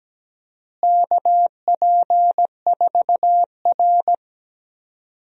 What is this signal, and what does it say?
Morse code, heard as a keyed tone, about 17 wpm, 700 Hz. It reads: KP4R
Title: DXCC-100 Call Signs - 4 Characters - Repeated 3x - 17wpm